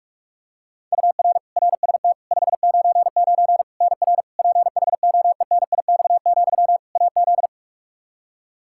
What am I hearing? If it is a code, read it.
Morse code, heard as a keyed tone, about 41 wpm, 700 Hz. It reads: UR RST 599 NR PHOENIX, AZ
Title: Practice QSO - Exchange 1 - Beginning - 45wpm